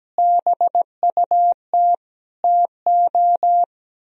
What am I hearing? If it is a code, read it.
Morse code, heard as a keyed tone, about 17 wpm, 700 Hz. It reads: BUT TO
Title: Speed-Racing - Sets of 5 Words 17wpm